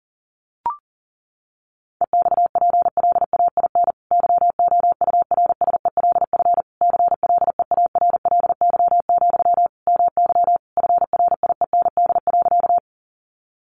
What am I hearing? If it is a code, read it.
Morse code, heard as a keyed tone, about 40 wpm, 700 Hz. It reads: EXPLAIN YOURSELF CLEARLY, MY FRIEND.
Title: Sentences from Top 1300 Words 40wpm